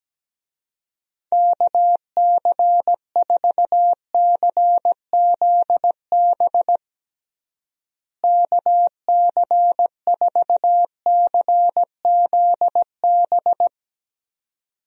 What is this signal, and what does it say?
Morse code, heard as a keyed tone, about 17 wpm, 700 Hz. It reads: KC4CZB KC4CZB
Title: USA Call Signs - 6 Characters - Repeated 2x - 17wpm